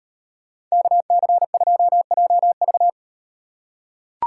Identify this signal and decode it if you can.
Morse code, heard as a keyed tone, about 38 wpm, 700 Hz. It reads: KC2JV